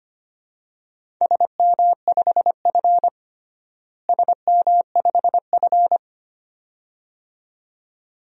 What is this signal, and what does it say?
Morse code, heard as a keyed tone, about 25 wpm, 700 Hz. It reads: SM5F SM5F
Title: DXCC-100 Call Signs - 4 Characters - Repeated 2x - 25wpm